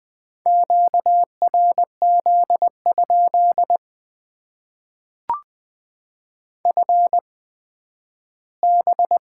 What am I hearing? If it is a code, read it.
Morse code, heard as a keyed tone, about 20 wpm, spaced as for 12 wpm, 700 Hz. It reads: QRZ? FB
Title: Top 90 QSO Elements - 8x Character Spacing - 20wpm